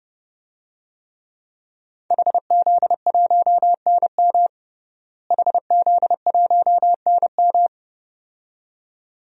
Morse code, heard as a keyed tone, about 30 wpm, 700 Hz. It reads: HZ1NM HZ1NM
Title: DXCC-100 Call Signs - 5 Characters - Repeated 2x - 30wpm